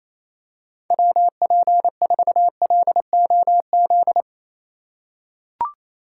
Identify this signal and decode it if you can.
Morse code, heard as a keyed tone, about 27 wpm, 700 Hz. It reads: WP4LOZ
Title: USA Call Signs - 6 Characters - 4x Character Spacing - 28wpm